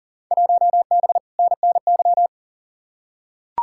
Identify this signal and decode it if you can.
Morse code, heard as a keyed tone, about 40 wpm, 700 Hz. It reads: 1B NNY
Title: Field Day Exchanges - 40wpm